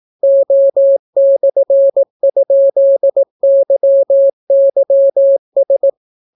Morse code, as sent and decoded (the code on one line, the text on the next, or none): --- -..-. ..--.. -.-- -.-- ...
O/?YYS